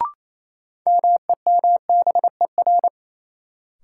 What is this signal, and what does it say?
Morse code, heard as a keyed tone, about 28 wpm, 700 Hz. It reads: MEMBER
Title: Sets of 3 Words - New Word Drill - 28wpm